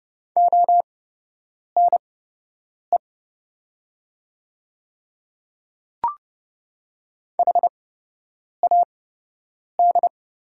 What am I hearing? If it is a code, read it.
Morse code, heard as a keyed tone, about 31 wpm, spaced as for 8 wpm, 700 Hz. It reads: ONE HAD
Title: 8x Character Spacing - Top 200 Words - 30wpm